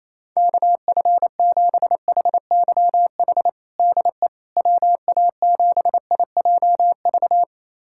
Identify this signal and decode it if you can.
Morse code, heard as a keyed tone, about 28 wpm, 700 Hz. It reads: KF7HYH DE WA7IJV
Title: USA Callsign DE Callsign - 6 Characters - 28wpm